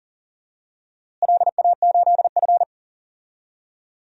Morse code, heard as a keyed tone, about 40 wpm, 700 Hz. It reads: LA8F
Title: DXCC-100 Call Signs - 4 Characters - Repeated 3x - 40wpm